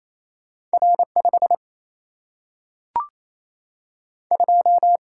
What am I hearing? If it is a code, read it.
Morse code, heard as a keyed tone, about 28 wpm, 700 Hz. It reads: R5 2
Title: Single-Letter Single-Number 28wpm